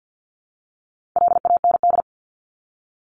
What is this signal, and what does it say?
Morse code, heard as a keyed tone, about 48 wpm, 700 Hz. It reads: LAND
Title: Top 300 Words 50wpm